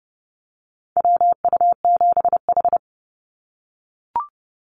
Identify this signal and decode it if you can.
Morse code, heard as a keyed tone, about 30 wpm, 700 Hz. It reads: WU7H
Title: NA SOTA Activator Callsigns 30wpm